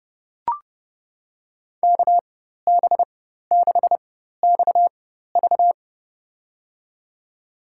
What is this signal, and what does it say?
Morse code, heard as a keyed tone, about 30 wpm, spaced as for 14 wpm, 700 Hz. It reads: KB6XV
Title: USA Call Signs - 5 Characters - 4x Character Spacing - 30wpm